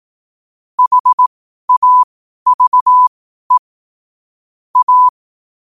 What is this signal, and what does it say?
Morse code, heard as a keyed tone, about 18 wpm, spaced as for 12 wpm, 1000 Hz. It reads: HAVE A
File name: qotdfiles / 12wpm